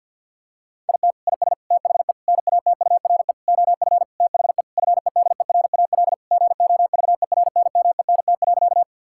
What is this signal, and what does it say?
Morse code, heard as a keyed tone, about 51 wpm, 700 Hz. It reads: IT IS THE NATURE OF THE FEDERAL GOVERNMENT.
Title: Sentences from Top 1000 Words - Review Part 5 - 50wpm